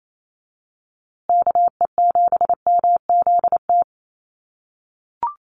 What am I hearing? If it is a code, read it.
Morse code, heard as a keyed tone, about 28 wpm, 700 Hz. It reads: KE7MZT
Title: USA Call Signs Repeated 2x 28wpm